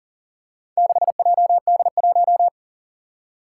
Morse code, heard as a keyed tone, about 40 wpm, 700 Hz. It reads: 6JD1
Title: Four Challenging Letter-Number Combinations 40wpm